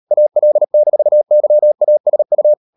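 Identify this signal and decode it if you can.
Morse code, heard as a keyed tone, about 38 wpm, 600 Hz. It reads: AL-YASU
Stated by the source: Morse code practice at 38 WPM, sending Ken Follett's novel Never in English, from CW med Gnister.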